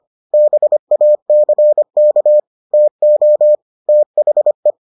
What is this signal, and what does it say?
Morse code, heard as a keyed tone, about 25 wpm, 600 Hz. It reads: BACK TO THE